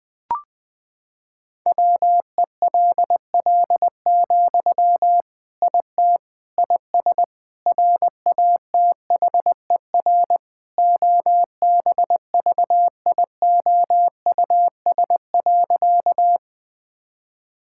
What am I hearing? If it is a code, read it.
Morse code, heard as a keyed tone, about 20 wpm, 700 Hz. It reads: WELL, IT IS RATHER OBVIOUS.
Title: Sentences from Top 1600 Words 20wpm